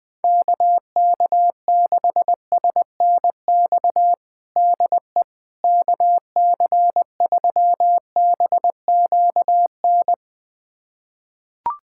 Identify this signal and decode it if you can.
Morse code, heard as a keyed tone, about 20 wpm, 700 Hz. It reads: KK6SNX DE KC3BQN
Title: USA Callsign DE Callsign - 6 Characters - 20wpm